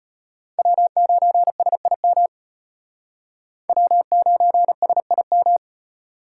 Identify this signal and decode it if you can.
Morse code, heard as a keyed tone, about 32 wpm, 700 Hz. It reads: W9SIM W9SIM